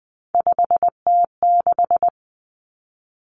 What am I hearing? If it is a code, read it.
Morse code, heard as a keyed tone, about 20 wpm, 700 Hz. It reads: 5T6
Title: Three Challenging Letter-Number Combinations 20wpm